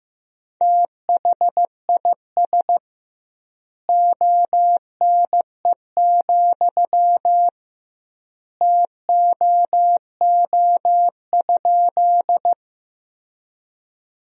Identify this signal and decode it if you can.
Morse code, heard as a keyed tone, about 15 wpm, 700 Hz. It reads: THIS ONE, TOO?
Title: 2x Word-Spacing - Sentences from Top 200 Words - 15wpm